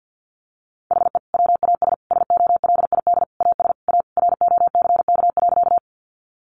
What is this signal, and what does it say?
Morse code, heard as a keyed tone, about 50 wpm, 700 Hz. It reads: HE WAS SOLID AS A ROCK.
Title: Sentences from Top 1500 Words 50wpm